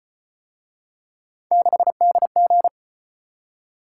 Morse code, heard as a keyed tone, about 35 wpm, 700 Hz. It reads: 6DG